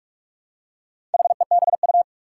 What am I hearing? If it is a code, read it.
Morse code, heard as a keyed tone, about 44 wpm, 700 Hz. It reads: HEBU
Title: Four Challenging Letter-Number Combinations 45wpm